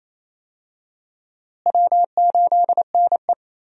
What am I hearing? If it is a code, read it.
Morse code, heard as a keyed tone, about 28 wpm, 700 Hz. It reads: W8NE
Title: USA Call Signs - 4 Characters - Repeated 3x - 28wpm